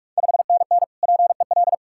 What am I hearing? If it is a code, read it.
Morse code, heard as a keyed tone, about 46 wpm, 690 Hz. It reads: 5NN PEL
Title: Ontario QSO Party Exchange 45wpm